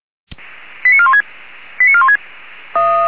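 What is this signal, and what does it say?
Five-tone selective call (ZVEI), heard as quick runs of five tones, tones of about 70 ms. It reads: 89416 89416
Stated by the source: Five-tone ZVEI selective call received on an SDR.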